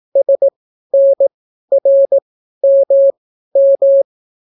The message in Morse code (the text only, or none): SNRMM